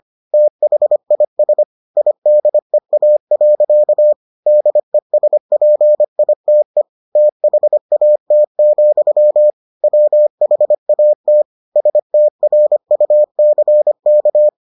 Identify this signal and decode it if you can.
Morse code, heard as a keyed tone, about 25 wpm, 600 Hz. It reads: THIS IDEA. DESPITE THAT, WHAT STRUCK